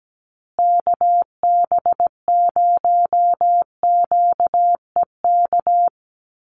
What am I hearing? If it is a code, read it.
Morse code, heard as a keyed tone, about 17 wpm, 700 Hz. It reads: KB0QEK
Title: USA Call Signs Repeated 2x 17wpm